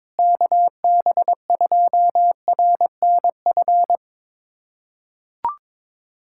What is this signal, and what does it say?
Morse code, heard as a keyed tone, about 22 wpm, 700 Hz. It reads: KB2RNF
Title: USA Callsign DE Callsign - 6 Characters - 22wpm